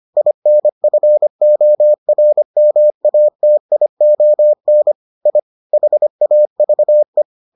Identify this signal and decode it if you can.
Morse code, heard as a keyed tone, about 25 wpm, 600 Hz. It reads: INFORMATION I HAVE